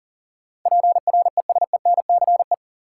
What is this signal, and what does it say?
Morse code, heard as a keyed tone, about 40 wpm, 700 Hz. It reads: PRESENCE
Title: Sets of 3 Words - New Word Drill - 40wpm